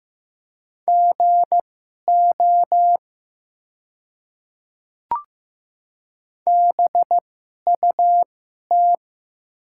Morse code, heard as a keyed tone, about 15 wpm, spaced as for 11 wpm, 700 Hz. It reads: GO BUT
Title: 2x Character Spacing - Top 100 Words - 15wpm